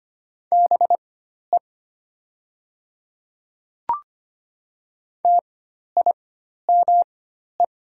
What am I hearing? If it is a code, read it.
Morse code, heard as a keyed tone, about 25 wpm, spaced as for 12 wpm, 700 Hz. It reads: BE TIME
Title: 4x Character Spacing - Top 100 Words - 25wpm